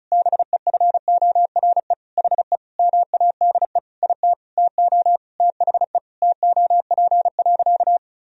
Morse code, heard as a keyed tone, about 36 wpm, 700 Hz. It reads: BEFORE HE MADE IT TO THE TOP.
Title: Sentences from Top 1000 Words - Review Part 2 - 35wpm